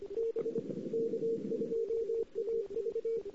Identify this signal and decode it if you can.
Morse code, heard as a keyed tone, about 25 wpm, 460 Hz. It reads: B K2AV
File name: CW / 2008-ss-cw-k5tr-op-n5rz -02:00